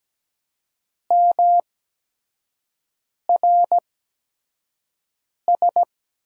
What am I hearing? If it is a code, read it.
Morse code, heard as a keyed tone, about 17 wpm, spaced as for 5 wpm, 700 Hz. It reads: MRS
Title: USA Call Signs - 6 Characters - 8x Character Spacing - 17wpm